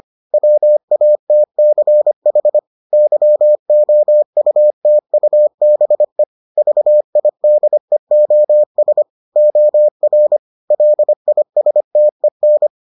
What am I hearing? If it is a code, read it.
Morse code, heard as a keyed tone, about 25 wpm, 600 Hz. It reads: WATCH YOUTUBE VIDEOS OR LISTEN